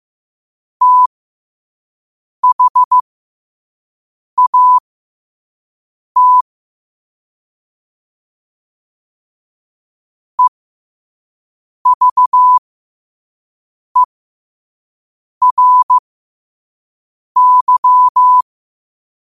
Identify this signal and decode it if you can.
Morse code, heard as a keyed tone, about 15 wpm, spaced as for 5 wpm, 1000 Hz. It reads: THAT EVERY